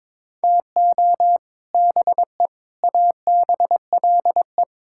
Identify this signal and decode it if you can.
Morse code, heard as a keyed tone, about 22 wpm, 700 Hz. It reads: TO BE ABLE
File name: practice / 22wpm / Sentences with 3-Word Phrases - 22wpm